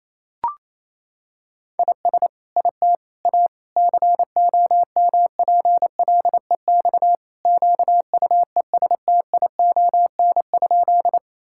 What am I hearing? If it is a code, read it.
Morse code, heard as a keyed tone, about 28 wpm, 700 Hz. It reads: IS IT A COMPLEX QUESTION?
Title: Sentences from Top 1000 Words - Review Part 6 - 28wpm